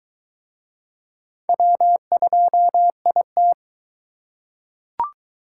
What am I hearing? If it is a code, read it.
Morse code, heard as a keyed tone, about 22 wpm, 700 Hz. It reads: W2IT